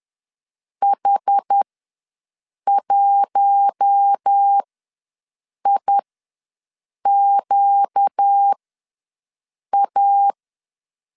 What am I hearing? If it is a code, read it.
Morse code, heard as a keyed tone, about 10 wpm, spaced as for 6 wpm, 800 Hz. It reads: H1IQA
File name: LEZIONI / 5 LBVC / 5 LBV ws 10wpm 3645